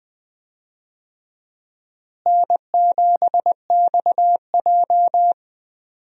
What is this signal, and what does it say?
Morse code, heard as a keyed tone, about 19 wpm, 700 Hz. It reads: N7XJ
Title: USA Call Signs - 4 Characters - Repeated 3x - 20wpm